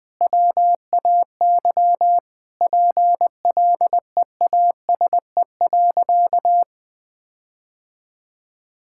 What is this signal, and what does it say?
Morse code, heard as a keyed tone, about 20 wpm, 700 Hz. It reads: WAY PLEASE.
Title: Sentences with 4-Word Phrases - Spoken Beginning - 20wpm